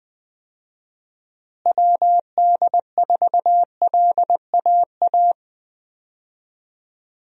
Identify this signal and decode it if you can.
Morse code, heard as a keyed tone, about 21 wpm, 700 Hz. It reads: WD4LAA
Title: USA Call Signs - 6 Characters - Repeated 3x - 20wpm